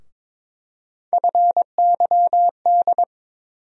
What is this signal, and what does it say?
Morse code, heard as a keyed tone, about 21 wpm, 700 Hz. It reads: FYD
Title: USA Call Sign Postfixes - 3 Letters 22wpm